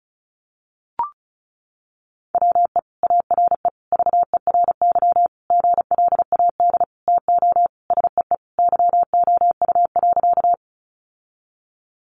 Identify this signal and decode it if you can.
Morse code, heard as a keyed tone, about 34 wpm, 700 Hz. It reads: WE ARE VERY GLAD TO SEE YOU.
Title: Sentences from Top 800 Words 35wpm